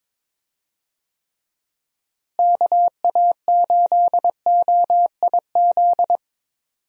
Morse code, heard as a keyed tone, about 22 wpm, 700 Hz. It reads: KA8OIZ
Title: USA Call Signs - 6 Characters - Repeated 3x - 22wpm